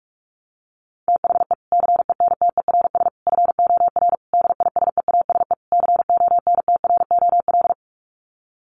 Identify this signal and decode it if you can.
Morse code, heard as a keyed tone, about 44 wpm, 700 Hz. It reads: THE CENTERS FOR DISEASE CONTROL
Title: Sets of 5 Words Encore2 45wpm